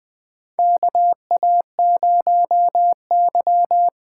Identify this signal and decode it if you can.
Morse code, heard as a keyed tone, about 20 wpm, 700 Hz. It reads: KA0Y
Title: USA Call Signs - 8x Character Spacing - 20wpm